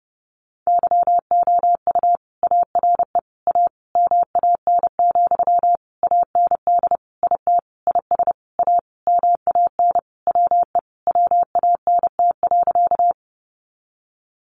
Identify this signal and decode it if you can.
Morse code, heard as a keyed tone, about 30 wpm, 700 Hz. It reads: YOU ARE A MAN, AND IT IS A MAN WE WANT.
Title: Sentences from Top 200 Words 30wpm